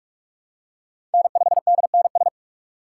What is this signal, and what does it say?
Morse code, heard as a keyed tone, about 44 wpm, 690 Hz. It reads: N5DNS
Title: USA Call Signs - 5 Characters - 6x Character Spacing - 45wpm